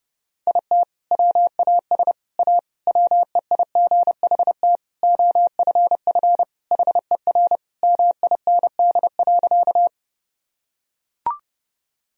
Morse code, heard as a keyed tone, about 30 wpm, 700 Hz. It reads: IT WAS A WEIGHT OFF HER MIND.
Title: Sentences from Top 1000 Words - Review Part 3 - 30wpm